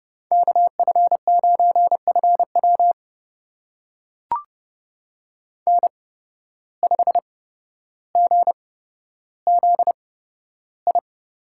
Morse code, heard as a keyed tone, about 31 wpm, spaced as for 13 wpm, 700 Hz. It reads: KF9FW N5GZI
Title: USA Call Signs - 5 Characters - 8x Character Spacing - 30wpm